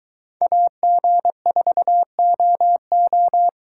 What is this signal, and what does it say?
Morse code, heard as a keyed tone, about 20 wpm, 700 Hz. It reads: AG4OO
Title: Speed Racing - USA 2x2 Call Signs 17wpm